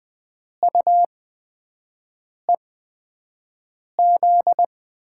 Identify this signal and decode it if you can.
Morse code, heard as a keyed tone, about 20 wpm, spaced as for 5 wpm, 700 Hz. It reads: UEZ
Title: USA Call Signs - 8x Character Spacing - 20wpm